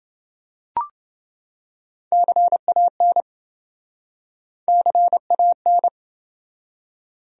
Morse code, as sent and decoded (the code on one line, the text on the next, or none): -.-. .- -. / -.-. .- -.
CAN CAN